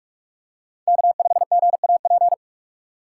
Morse code, heard as a keyed tone, about 46 wpm, 700 Hz. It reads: K5GAP
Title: USA Call Signs - 5 Characters - 4x Character Spacing - 45wpm